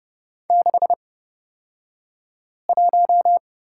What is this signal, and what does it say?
Morse code, heard as a keyed tone, about 29 wpm, spaced as for 5 wpm, 690 Hz. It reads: 61
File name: Lesson 36d - BT - Two Characters 30wpm